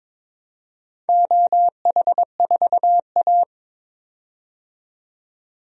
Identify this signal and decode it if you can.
Morse code, heard as a keyed tone, about 22 wpm, 700 Hz. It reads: OH4A
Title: DXCC-100 Call Signs - 4 Characters - 22wpm